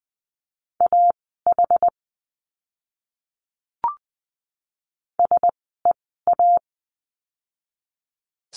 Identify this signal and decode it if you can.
Morse code, heard as a keyed tone, about 20 wpm, spaced as for 14 wpm, 700 Hz. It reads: AH SEA
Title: 061 Lesson 11f - Letter h - Words 20wpm